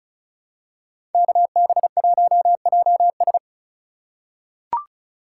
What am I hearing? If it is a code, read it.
Morse code, heard as a keyed tone, about 34 wpm, 700 Hz. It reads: KB1JS
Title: USA Call Signs - 2x Character Spacing - 35wpm